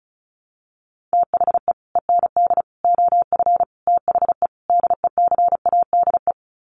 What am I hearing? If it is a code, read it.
Morse code, heard as a keyed tone, about 34 wpm, 700 Hz. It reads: THE END OF THE DECADE
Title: Sets of 5 Words Encore 35wpm